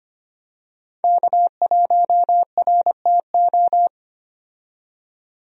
Morse code, heard as a keyed tone, about 25 wpm, 700 Hz. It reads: K1RTO
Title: USA Call Signs - 5 Characters - 25wpm